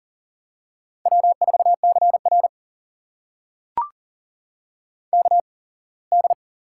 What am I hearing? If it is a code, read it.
Morse code, heard as a keyed tone, about 40 wpm, spaced as for 24 wpm, 700 Hz. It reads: W4CR KD
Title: USA Call Signs - 8x Character Spacing - 40wpm